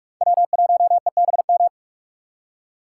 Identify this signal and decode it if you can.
Morse code, heard as a keyed tone, about 46 wpm, 700 Hz. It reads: W1EBM
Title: K1USN SST QSO Practice - 45wpm